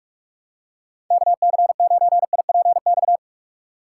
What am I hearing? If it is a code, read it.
Morse code, heard as a keyed tone, about 45 wpm, 700 Hz. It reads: KC9IPX
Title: USA Call Signs - 6 Characters - 6x Character Spacing - 45wpm